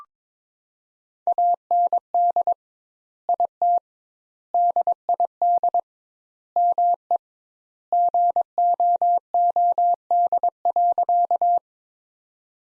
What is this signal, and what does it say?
Morse code, heard as a keyed tone, about 22 wpm, 700 Hz. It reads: AND IT DID ME GOOD.
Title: Sentences Top 200 Words - 22wpm